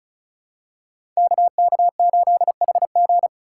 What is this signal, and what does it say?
Morse code, heard as a keyed tone, about 35 wpm, 700 Hz. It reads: KK8HG
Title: Code Groups - 5 Letter:Numbers - 35wpm